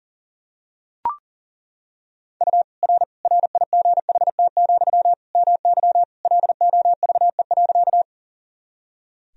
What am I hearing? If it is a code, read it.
Morse code, heard as a keyed tone, about 37 wpm, 700 Hz. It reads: U R RIGHT, MY LOVE.
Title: Sentences from Top 400 Words and Abbreviations 40wpm